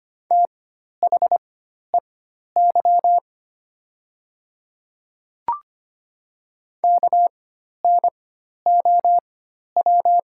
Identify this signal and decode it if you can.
Morse code, heard as a keyed tone, about 25 wpm, spaced as for 12 wpm, 700 Hz. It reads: THEY KNOW
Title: 4x Character Spacing - Top 300 Words - 25wpm